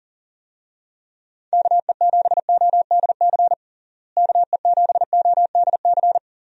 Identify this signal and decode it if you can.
Morse code, heard as a keyed tone, about 40 wpm, 700 Hz. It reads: KE7ODC KE7ODC
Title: USA Call Signs - 6 Characters - Repeated 3x - 40wpm